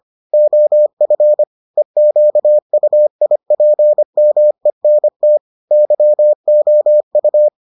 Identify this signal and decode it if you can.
Morse code, heard as a keyed tone, about 25 wpm, 600 Hz. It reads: OF EQUIPMENT YOU